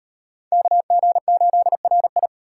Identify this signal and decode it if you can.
Morse code, heard as a keyed tone, about 38 wpm, 700 Hz. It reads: KG8RI